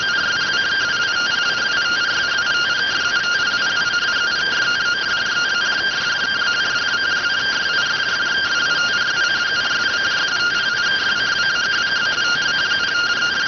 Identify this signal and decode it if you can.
Two-tone teleprinter signal, about 170 Hz shift.